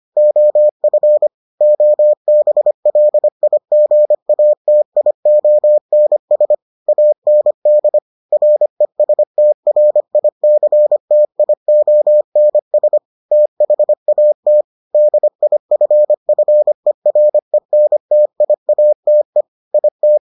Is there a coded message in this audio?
Yes: Morse code from 0.2 to 20.2 s, heard as a keyed tone, about 25 wpm, 600 Hz. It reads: OF OBLIGATIONS AND RESTRICTIONS THAT DIFFERENTIATE IT